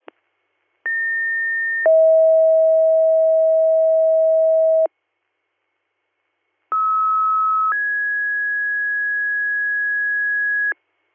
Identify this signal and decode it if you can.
Two-tone page, heard as a one-second tone, then a three-second tone, about 1800 then 640 Hz, about 1290 then 1650 Hz. It is heard from 0.9 to 10.7 s.